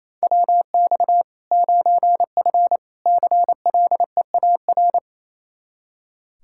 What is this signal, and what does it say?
Morse code, heard as a keyed tone, about 28 wpm, 700 Hz. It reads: WX 9F CLEAR
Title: K1USN SST QSO Practice 28wpm